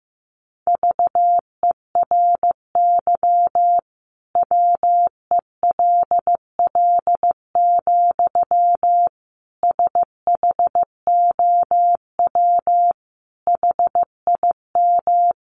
Morse code, heard as a keyed tone, about 15 wpm, 700 Hz. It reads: VERY WELL, SHOW HIM
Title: Sentences from Top 200 Words 15wpm